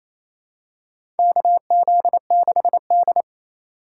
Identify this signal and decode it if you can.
Morse code, heard as a keyed tone, about 28 wpm, 700 Hz. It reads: KZ6D
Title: USA Call Signs - 4 Characters - 2x Character Spacing - 28wpm